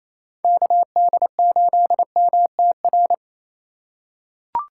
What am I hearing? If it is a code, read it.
Morse code, heard as a keyed tone, about 28 wpm, 700 Hz. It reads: KD8MTR
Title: USA Call Signs - 6 Characters - 4x Character Spacing - 28wpm